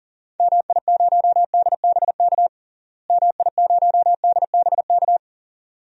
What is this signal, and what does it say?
Morse code, heard as a keyed tone, about 37 wpm, 700 Hz. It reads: MI0DBK MI0DBK
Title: DXCC-100 Call Signs - 6 Characters - Repeated 2x - 40wpm